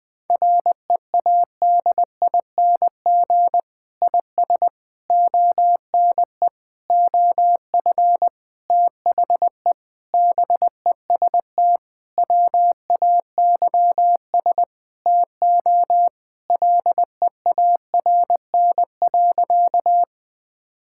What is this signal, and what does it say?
Morse code, heard as a keyed tone, about 20 wpm, 700 Hz. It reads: READING IS ONE OF THE BEST WAYS TO LEARN.
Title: Sentences with 6-Word Phrases - Spoken Beginning - 20wpm